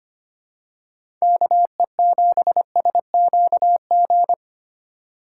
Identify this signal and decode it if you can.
Morse code, heard as a keyed tone, about 25 wpm, 700 Hz. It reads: KE7SQG
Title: USA Call Signs - 6 Characters - 4x Character Spacing - 25wpm